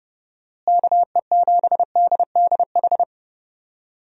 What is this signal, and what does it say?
Morse code, heard as a keyed tone, about 30 wpm, 700 Hz. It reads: KE7DDH
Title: USA Call Signs - 6 Characters - Repeated 3x - 30wpm